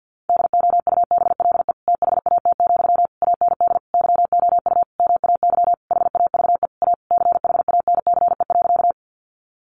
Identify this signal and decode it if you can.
Morse code, heard as a keyed tone, about 50 wpm, 700 Hz. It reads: DOUBLE THAT, AND YOU MAY HAVE A CHANCE.
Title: Sentences from Top 1400 Words 50wpm